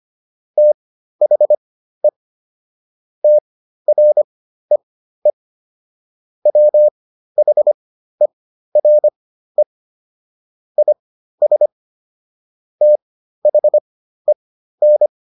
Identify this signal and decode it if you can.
Morse code, heard as a keyed tone, about 25 wpm, spaced as for 13 wpm, 600 Hz. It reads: THE TREE WHERE IS THEN